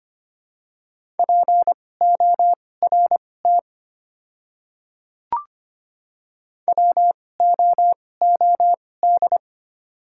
Morse code, heard as a keyed tone, about 25 wpm, spaced as for 18 wpm, 700 Hz. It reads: PORT WOOD
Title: 2x Character Spacing - Top 300 Words - 25wpm